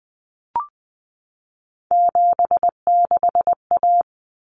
Morse code, heard as a keyed tone, about 20 wpm, 700 Hz. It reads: 76A
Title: Common Cut Numbers - 3-Digit 20wpm